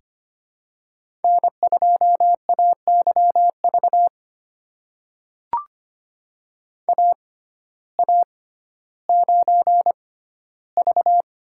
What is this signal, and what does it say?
Morse code, heard as a keyed tone, about 25 wpm, spaced as for 14 wpm, 700 Hz. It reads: N2AYV AA9V
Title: USA Call Signs - 5 Characters - 6x Character Spacing - 25wpm